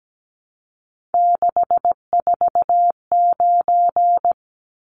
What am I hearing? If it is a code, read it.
Morse code, heard as a keyed tone, about 17 wpm, 700 Hz. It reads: 649
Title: Numbers - 3-Digit 17wpm